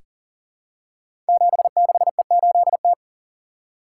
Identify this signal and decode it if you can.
Morse code, heard as a keyed tone, about 40 wpm, 700 Hz. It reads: 76E8T
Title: Code Groups - 5 Letter:Numbers - 40wpm